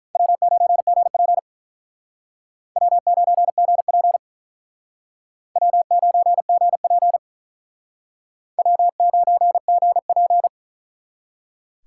Morse code, heard as a keyed tone, about 42 wpm, 700 Hz. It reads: W9GP W9GP W9GP W9GP
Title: Speed Racing - USA 1x2 Call Signs 35wpm